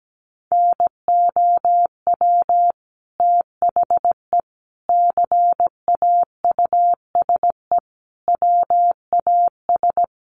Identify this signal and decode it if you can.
Morse code, heard as a keyed tone, about 17 wpm, 700 Hz. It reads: NOW THE CAUSE WAS
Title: Sentences from Top 1500 Words 17wpm